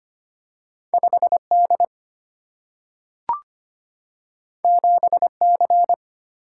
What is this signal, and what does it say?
Morse code, heard as a keyed tone, about 25 wpm, 700 Hz. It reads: 5 D 7 C
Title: Single-Letter Single-Number 25wpm